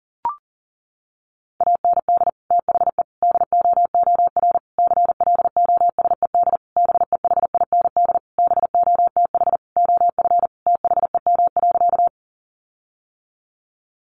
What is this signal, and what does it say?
Morse code, heard as a keyed tone, about 40 wpm, 700 Hz. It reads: AND THE DOOR CLOSED BEHIND BOTH OF THEM.
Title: Sentences from Top 1400 Words 40wpm